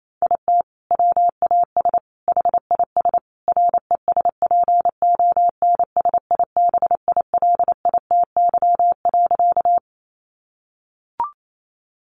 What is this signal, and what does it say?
Morse code, heard as a keyed tone, about 28 wpm, 700 Hz. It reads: IT WAS HIS RESPONSIBILITY.
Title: Sentences from Top 1200 Words 28wpm